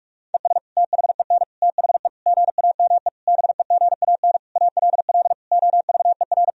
Morse code, heard as a keyed tone, about 41 wpm, 700 Hz. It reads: ES THEN THE GAME BEGAN ALL OVER
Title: Sentences from Top 500 Words and Abbreviations 45wpm